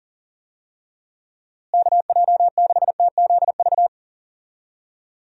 Morse code, heard as a keyed tone, about 40 wpm, 700 Hz. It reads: KJ6TZV
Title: USA Call Signs - 6 Characters - Repeated 2x - 40wpm